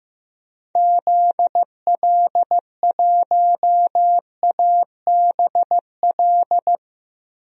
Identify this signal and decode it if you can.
Morse code, heard as a keyed tone, about 15 wpm, 700 Hz. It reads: ZL1ABL
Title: DXCC-100 Call Signs - 6 Characters - 15wpm